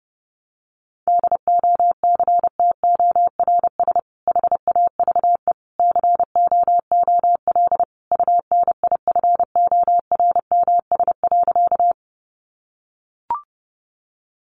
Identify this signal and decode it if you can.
Morse code, heard as a keyed tone, about 30 wpm, 700 Hz. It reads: DOCTORS HAVE COOL UNIFORMS.
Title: Easy Phrases 30wpm